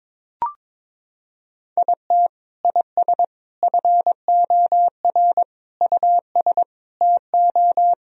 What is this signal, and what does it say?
Morse code, heard as a keyed tone, about 22 wpm, 700 Hz. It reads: IT IS FOR US TO
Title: Sentences from Top 1000 Words - Review Part 4 - 22wpm